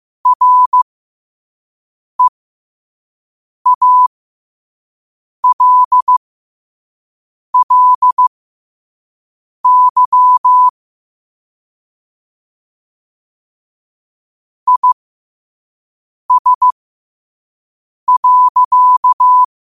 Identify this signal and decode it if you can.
Morse code, heard as a keyed tone, about 15 wpm, spaced as for 5 wpm, 1000 Hz. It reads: REALLY IS.
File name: 2026-02-15-5wpm